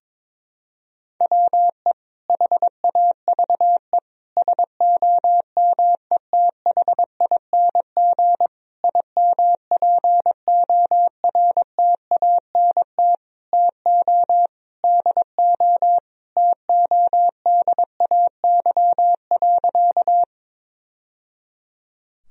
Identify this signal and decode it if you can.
Morse code, heard as a keyed tone, about 22 wpm, 700 Hz. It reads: WE HAVE SOMETHING IMPORTANT TO DO TODAY.
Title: Sentences from Top 1000 Words - Review Part 4 - 22wpm